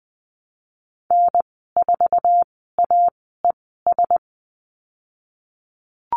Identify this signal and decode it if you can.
Morse code, heard as a keyed tone, about 20 wpm, spaced as for 15 wpm, 700 Hz. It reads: N4AES
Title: 062 Lesson 11g - Letter h - Call Signs 20wpm